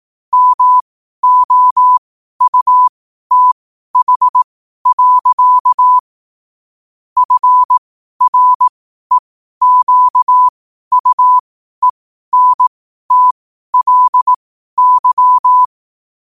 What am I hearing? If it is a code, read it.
Morse code, heard as a keyed tone, about 18 wpm, spaced as for 12 wpm, 1000 Hz. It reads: MOUTH. FREQUENTLY
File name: qotdfiles / 12wpm